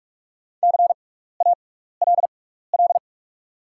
Morse code, heard as a keyed tone, about 44 wpm, spaced as for 15 wpm, 700 Hz. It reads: CALL
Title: 6x Character Spacing - Top 100 Words - 45wpm